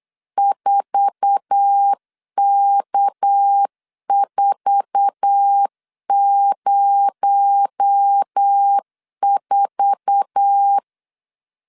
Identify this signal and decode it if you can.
Morse code, heard as a keyed tone, about 8 wpm, 800 Hz. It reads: 4K404
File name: LEZIONI / 8 YJ45 / 8 YJ4 8wpm 3246